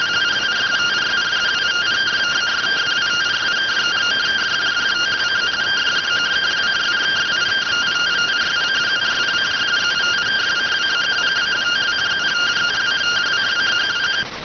Two-tone teleprinter signal, about 170 Hz shift.